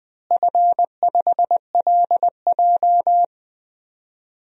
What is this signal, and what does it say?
Morse code, heard as a keyed tone, about 21 wpm, 700 Hz. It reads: F5LJ
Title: DXCC-100 Call Signs - 4 Characters - Repeated 2x - 20wpm